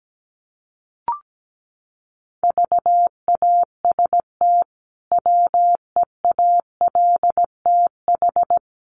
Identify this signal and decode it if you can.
Morse code, heard as a keyed tone, about 17 wpm, 700 Hz. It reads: VAST WEALTH
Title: Sentences from Top 1300 Words 17wpm